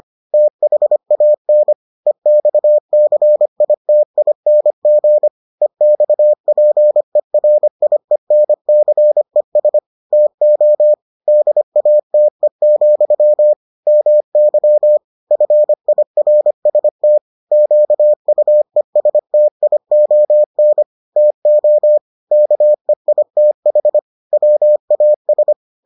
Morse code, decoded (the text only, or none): THAN EXCITING EXPERIENCES TO DATE, MY FIRST QUESTION TO KEITH WAS